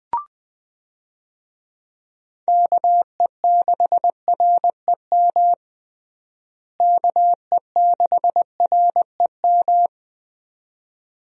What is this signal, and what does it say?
Morse code, heard as a keyed tone, about 21 wpm, 700 Hz. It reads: KE6REM KE6REM
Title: USA Call Signs - 6 Characters - Repeated 2x - 20wpm